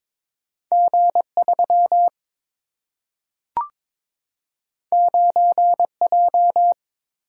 Morse code, heard as a keyed tone, about 22 wpm, 700 Hz. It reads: G3 9J
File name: practice / 22wpm / Two Letter-Number Combinations - 22wpm